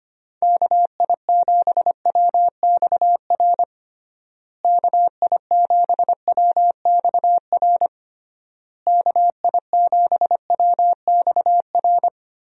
Morse code, heard as a keyed tone, about 25 wpm, 700 Hz. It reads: KI7WXR KI7WXR KI7WXR
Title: USA Call Signs - 6 Characters - Repeated 3x - 25wpm